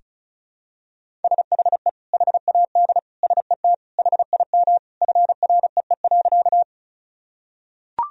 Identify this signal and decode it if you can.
Morse code, heard as a keyed tone, about 36 wpm, 700 Hz. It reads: SHE HAD SET HIM FREE.
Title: Sentences from Top 1000 Words - Review Part 1 - 35wpm